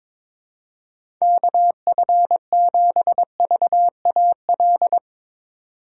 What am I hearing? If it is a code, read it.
Morse code, heard as a keyed tone, about 22 wpm, 700 Hz. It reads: KF7VAL
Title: USA Call Signs - 6 Characters - 3x Character Spacing - 22wpm